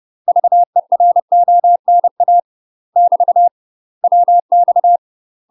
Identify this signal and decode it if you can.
Morse code, heard as a keyed tone, about 30 wpm, 700 Hz. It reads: VERONA <BT> WX
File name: UK MASTER QSOs TRAINING / 30WPM / UK MASTER QSOs 0026